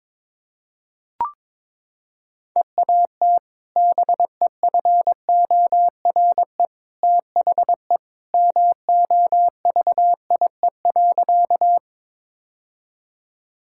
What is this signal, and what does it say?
Morse code, heard as a keyed tone, about 22 wpm, 700 Hz. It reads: EAT BEFORE THE MOVIE.
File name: practice / 22wpm / Sentences with 2-Word Phrases - 22wpm